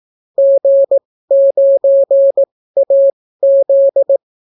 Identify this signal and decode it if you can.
Morse code, heard as a keyed tone, about 18 wpm, spaced as for 14 wpm, 550 Hz. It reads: G9AZ